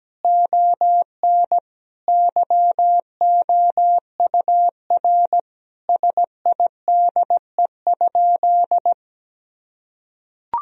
Morse code, heard as a keyed tone, about 17 wpm, 700 Hz. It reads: ON YOUR SIDE?
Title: Sentences from Top 1000 Words - Review Part 3 - 17wpm